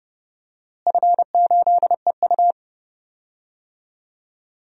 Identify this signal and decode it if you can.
Morse code, heard as a keyed tone, about 30 wpm, 700 Hz. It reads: F8EU
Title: DXCC-100 Call Signs - 4 Characters - Repeated 2x - 30wpm